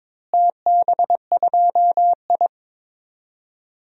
Morse code, heard as a keyed tone, about 22 wpm, 700 Hz. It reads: TB2I
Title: DXCC-100 Call Signs - 4 Characters - Repeated 3x - 22wpm